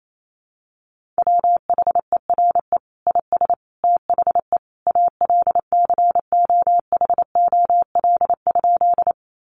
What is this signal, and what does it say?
Morse code, heard as a keyed tone, about 28 wpm, 700 Hz. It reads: WHERE IS THE ALCOHOL?
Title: Sentences from Top 1400 Words 28wpm